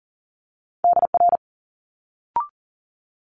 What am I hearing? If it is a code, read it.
Morse code, heard as a keyed tone, about 40 wpm, 700 Hz. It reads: DR
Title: Multi-letter Phonemes Quick Fire 40wpm